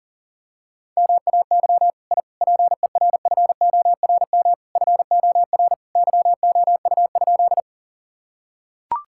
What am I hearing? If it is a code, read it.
Morse code, heard as a keyed tone, about 40 wpm, 700 Hz. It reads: MAY I PERFORM FOR YOU?
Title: Sentences from Top 1000 Words - Review Part 5 - 40wpm